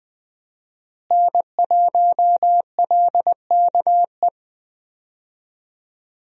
Morse code, heard as a keyed tone, about 20 wpm, 700 Hz. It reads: N1LKE
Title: USA Call Signs - 5 Characters - Repeated 2x - 20wpm